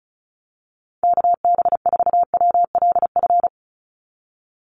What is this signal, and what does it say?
Morse code, heard as a keyed tone, about 36 wpm, 700 Hz. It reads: KB4WLF
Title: Call Signs - 35wpm